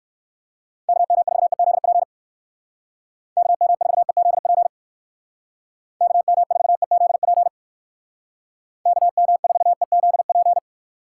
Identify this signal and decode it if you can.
Morse code, heard as a keyed tone, about 59 wpm, 700 Hz. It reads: KM4EZP KM4EZP KM4EZP KM4EZP